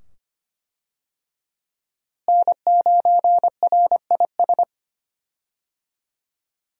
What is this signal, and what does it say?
Morse code, heard as a keyed tone, about 25 wpm, 700 Hz. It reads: N9RIS
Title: USA Call Signs - 5 Characters - Repeated 2x - 25wpm